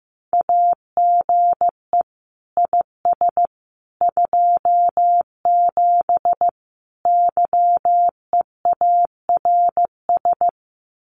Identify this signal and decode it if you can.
Morse code, heard as a keyed tone, about 15 wpm, 700 Hz. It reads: AGE IS 27 YEARS
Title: Common QSO Phrases Encore 15wpm